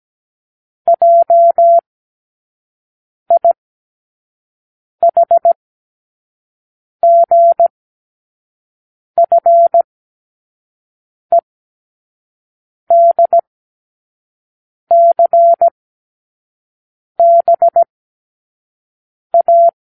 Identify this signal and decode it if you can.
Morse code, heard as a keyed tone, about 17 wpm, spaced as for 5 wpm, 680 Hz. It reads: JIHGFEDCBA
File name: k7qo / Lesson15(A-J)